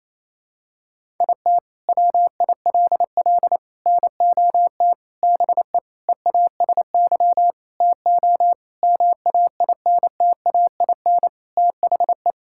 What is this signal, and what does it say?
Morse code, heard as a keyed tone, about 28 wpm, 700 Hz. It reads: IT WILL NOT BE EASY TO MAINTAIN THE
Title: Sentences from Top 1000 Words - Review Part 5 - 28wpm